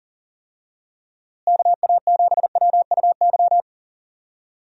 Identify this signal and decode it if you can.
Morse code, heard as a keyed tone, about 40 wpm, 700 Hz. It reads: KA7WUY
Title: USA Call Signs - 6 Characters - Repeated 2x - 40wpm